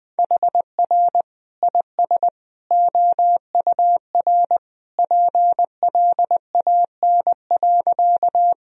Morse code, heard as a keyed tone, about 20 wpm, 700 Hz. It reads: HR IS OUR PLAN.
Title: Sentences from Top 500 Words and Abbreviations 20wpm